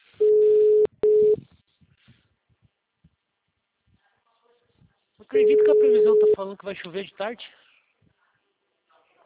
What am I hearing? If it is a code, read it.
Ringback tone (Europe, 425 Hz), with speech.
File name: out- -2007- - - .101